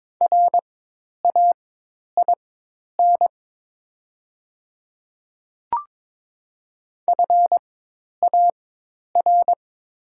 Morse code, heard as a keyed tone, about 22 wpm, spaced as for 10 wpm, 700 Hz. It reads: RAIN FAR
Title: 4x Character Spacing - Top 300 Words - 22wpm